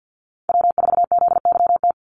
Morse code, heard as a keyed tone, about 50 wpm, 700 Hz. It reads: W4ZYT
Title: Call Signs - 50wpm